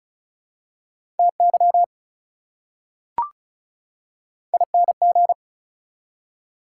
Morse code, heard as a keyed tone, about 36 wpm, 700 Hz. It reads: TY ING
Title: Prefixes and Suffixes 35wpm